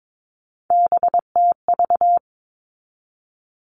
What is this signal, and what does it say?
Morse code, heard as a keyed tone, about 22 wpm, 700 Hz. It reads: BTV
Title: All Cut Numbers - 3-Digit 22wpm